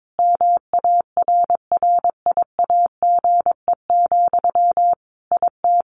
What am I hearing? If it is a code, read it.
Morse code, heard as a keyed tone, about 22 wpm, 700 Hz. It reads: MARRIAGE, IT
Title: Sentences from Top 1500 Words 22wpm